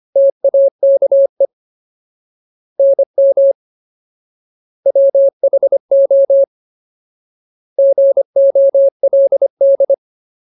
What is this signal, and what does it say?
Morse code, heard as a keyed tone, about 25 wpm, 550 Hz. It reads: TAKE NM WHO GOLD
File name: Ultimate Word List / 25WPM Ultimate